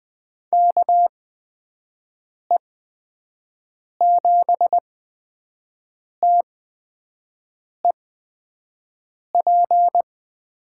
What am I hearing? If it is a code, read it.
Morse code, heard as a keyed tone, about 20 wpm, spaced as for 5 wpm, 700 Hz. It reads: KE7TEP
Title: USA Call Signs - 8x Character Spacing - 20wpm